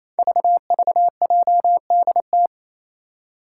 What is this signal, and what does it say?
Morse code, heard as a keyed tone, about 27 wpm, 700 Hz. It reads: VVJDT
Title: Five Challenging Letter-Number Combinations 28wpm